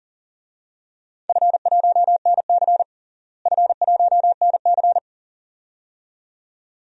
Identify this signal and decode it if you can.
Morse code, heard as a keyed tone, about 40 wpm, 700 Hz. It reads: F1NC F1NC
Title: DXCC-100 Call Signs - 4 Characters - Repeated 2x - 40wpm